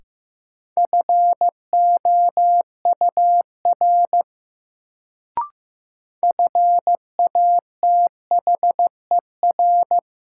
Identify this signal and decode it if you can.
Morse code, heard as a keyed tone, about 15 wpm, 700 Hz. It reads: FOUR FATHER
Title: Sentences from Top 300 Words - New Word Drill - 15wpm